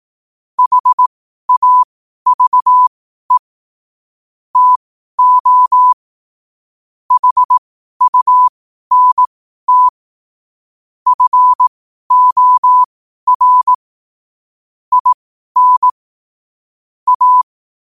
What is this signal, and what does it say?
Morse code, heard as a keyed tone, about 18 wpm, spaced as for 12 wpm, 1000 Hz. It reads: HAVE TO HUNT FOR IN A
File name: qotdfiles / 12wpm